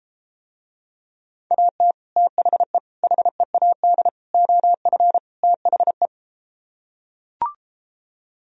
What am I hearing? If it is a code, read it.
Morse code, heard as a keyed tone, about 33 wpm, 700 Hz. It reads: AT THE HEAD OF THE